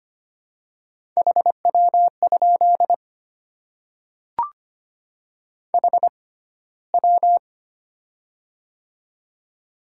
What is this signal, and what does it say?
Morse code, heard as a keyed tone, about 25 wpm, spaced as for 15 wpm, 700 Hz. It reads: HW? HW
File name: Top 90 QSO Elements - 6x Character Spacing - 25wpm